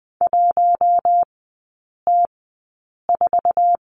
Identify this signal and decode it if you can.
Morse code, heard as a keyed tone, about 20 wpm, spaced as for 8 wpm, 700 Hz. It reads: 1T4
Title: 087 Lesson 15e - Number 5 - Three Characters 20wpm